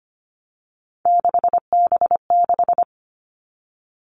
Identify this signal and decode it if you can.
Morse code, heard as a keyed tone, about 25 wpm, 700 Hz. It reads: 6B6
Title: All Cut Numbers - 3-Digit 25wpm